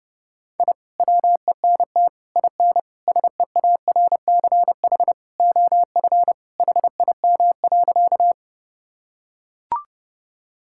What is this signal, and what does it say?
Morse code, heard as a keyed tone, about 30 wpm, 700 Hz. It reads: I WENT IN SEARCH OF HIM.
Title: Sentences from Top 1000 Words - Review Part 5 - 30wpm